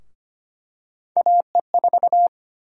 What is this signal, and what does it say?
Morse code, heard as a keyed tone, about 25 wpm, 700 Hz. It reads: AE4
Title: USA Call Sign Prefixes - Repeated 2x - 25wpm